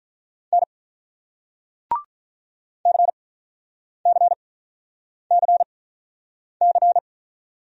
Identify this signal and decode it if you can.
Morse code, heard as a keyed tone, about 47 wpm, spaced as for 9 wpm, 700 Hz. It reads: N CCCC